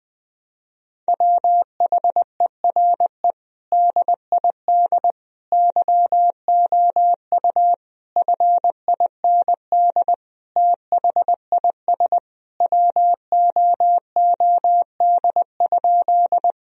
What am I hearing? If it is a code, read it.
Morse code, heard as a keyed tone, about 20 wpm, 700 Hz. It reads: WHERE DID YOU FIND THIS WOOD?
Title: Sentences from Top 1000 Words - Review Part 2 - 20wpm